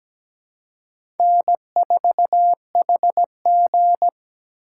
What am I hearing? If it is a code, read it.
Morse code, heard as a keyed tone, about 17 wpm, 700 Hz. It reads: N4HG
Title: USA Call Signs - 3x Character Spacing - 17wpm